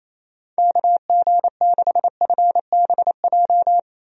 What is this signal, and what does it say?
Morse code, heard as a keyed tone, about 28 wpm, 700 Hz. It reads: KG6FBJ
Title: USA Call Signs - 6 Characters - Repeated 2x - 28wpm